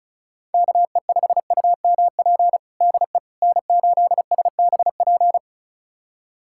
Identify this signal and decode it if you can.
Morse code, heard as a keyed tone, about 36 wpm, 700 Hz. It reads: KE5UMP DE N8SBP
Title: USA Callsign DE Callsign 35wpm